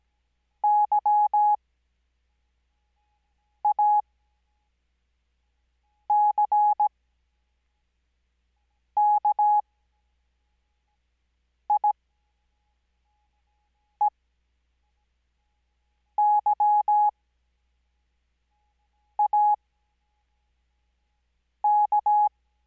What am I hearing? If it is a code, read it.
Morse code, heard as a keyed tone, about 17 wpm, spaced as for 4 wpm, 840 Hz. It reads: YACKIEYAK
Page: Farnsworth Spacing: 15WPM Character speed Spaced at 5WPM.